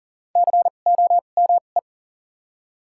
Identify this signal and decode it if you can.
Morse code, heard as a keyed tone, about 40 wpm, spaced as for 29 wpm, 700 Hz. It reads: COME
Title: 2x Character Spacing - Top 100 Words - 40wpm